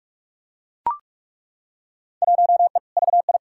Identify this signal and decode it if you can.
Morse code, heard as a keyed tone, about 43 wpm, 700 Hz. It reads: 1E VI
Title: Field Day Exchanges Encore - 45wpm